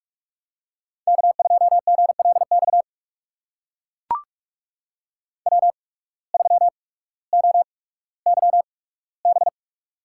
Morse code, heard as a keyed tone, about 44 wpm, spaced as for 19 wpm, 700 Hz. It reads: K2GLX W3OYB
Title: USA Call Signs - 5 Characters - 8x Character Spacing - 45wpm